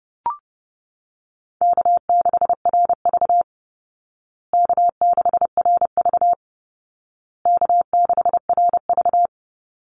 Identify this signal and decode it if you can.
Morse code, heard as a keyed tone, about 30 wpm, 700 Hz. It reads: K6RV K6RV K6RV
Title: Call Signs Repeated 2x - 30wpm